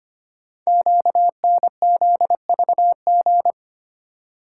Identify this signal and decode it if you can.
Morse code, heard as a keyed tone, about 25 wpm, 700 Hz. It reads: QNZVG
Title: Code Groups - 5 Letters - 25wpm